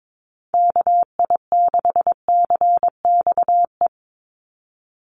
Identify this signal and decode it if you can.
Morse code, heard as a keyed tone, about 22 wpm, 700 Hz. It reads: KI6CXE
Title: USA Call Signs 22wpm